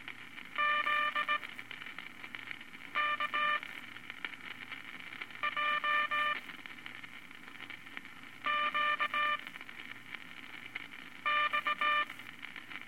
Morse code, heard as a keyed tone, about 18 wpm, spaced as for 4 wpm, 1190 Hz, with noise. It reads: ZKJQX
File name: Morse Code Records/15A LCM 158 Units I-V Groups of Five